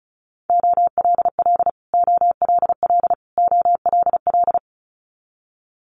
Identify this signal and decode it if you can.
Morse code, heard as a keyed tone, about 36 wpm, 700 Hz. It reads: OLL OLL OLL
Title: Multi-letter Phonemes Repeated 3x 35wpm